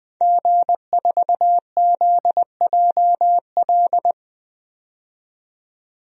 Morse code, heard as a keyed tone, about 20 wpm, 700 Hz. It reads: G4ZJL
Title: DXCC-100 Call Signs - 5 Characters - 20wpm